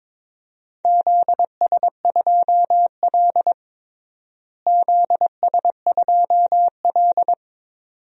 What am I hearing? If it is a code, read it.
Morse code, heard as a keyed tone, about 21 wpm, 700 Hz. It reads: ZS2L ZS2L
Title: DXCC-100 Call Signs - 4 Characters - Repeated 2x - 22wpm